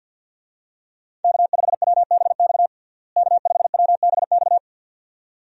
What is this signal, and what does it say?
Morse code, heard as a keyed tone, about 47 wpm, 700 Hz. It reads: K5WBX K5WBX
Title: USA Call Signs - 5 Characters - Repeated 2x - 50wpm